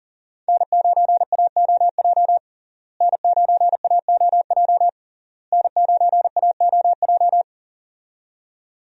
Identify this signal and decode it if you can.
Morse code, heard as a keyed tone, about 40 wpm, 700 Hz. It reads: N9AOJ N9AOJ N9AOJ
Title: USA Call Signs - 5 Characters - Repeated 3x - 40wpm